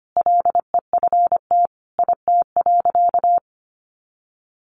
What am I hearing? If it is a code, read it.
Morse code, heard as a keyed tone, about 25 wpm, 700 Hz. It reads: LEFT IT.
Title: A Sound Of Thunder 25wpm